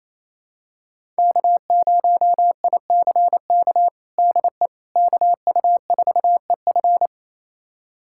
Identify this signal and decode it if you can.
Morse code, heard as a keyed tone, about 27 wpm, 700 Hz. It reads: K0ICK DE KU4EF
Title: USA Callsign DE Callsign - 5 Characters - 28wpm